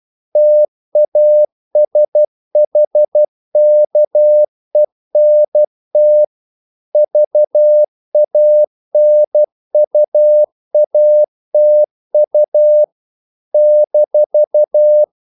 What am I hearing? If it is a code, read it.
Morse code, heard as a keyed tone, about 12 wpm, 600 Hz. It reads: TASHKENT VANUATU -
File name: CW / Practice / NationalCapitals12